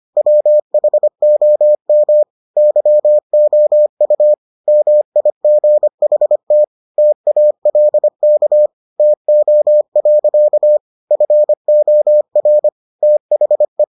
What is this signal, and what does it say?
Morse code, heard as a keyed tone, about 25 wpm, 600 Hz. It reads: WHOM YOU MIGHT TALK TO. FOR THE